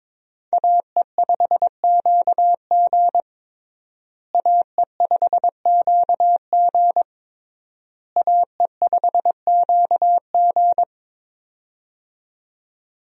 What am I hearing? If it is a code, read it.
Morse code, heard as a keyed tone, about 22 wpm, 700 Hz. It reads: AE5QG AE5QG AE5QG
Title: USA Call Signs - 5 Characters - Repeated 3x - 22wpm